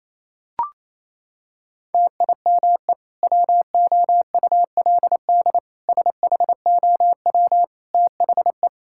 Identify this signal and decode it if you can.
Morse code, heard as a keyed tone, about 28 wpm, 700 Hz. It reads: TIME WOULD SHOW THE
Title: Sentences from Top 1000 Words - Review Part 5 - 28wpm